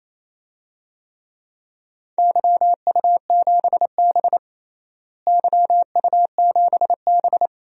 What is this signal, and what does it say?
Morse code, heard as a keyed tone, about 28 wpm, 700 Hz. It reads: YU7B YU7B
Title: DXCC-100 Call Signs - 4 Characters - Repeated 2x - 28wpm